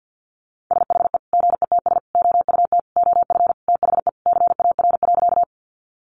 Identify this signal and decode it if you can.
Morse code, heard as a keyed tone, about 50 wpm, 700 Hz. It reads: SHE GETS OUT OF THE CAR.
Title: Easy Phrases 50wpm